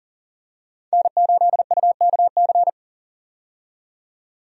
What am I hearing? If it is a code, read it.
Morse code, heard as a keyed tone, about 40 wpm, 700 Hz. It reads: N8UKC
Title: USA Call Signs - 5 Characters - 40wpm